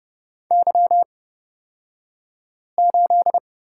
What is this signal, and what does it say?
Morse code, heard as a keyed tone, about 30 wpm, spaced as for 5 wpm, 700 Hz. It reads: Y8
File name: Lesson 36d - BT - Two Characters 30wpm